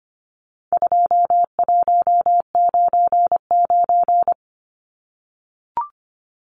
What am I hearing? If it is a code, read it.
Morse code, heard as a keyed tone, about 25 wpm, 700 Hz. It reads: 2199
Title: Numbers - 4-Digit 25wpm